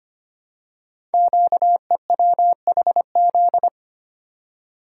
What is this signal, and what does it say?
Morse code, heard as a keyed tone, about 25 wpm, 700 Hz. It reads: QEWHZ
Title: Code Groups - 5 Letter:Numbers - 25wpm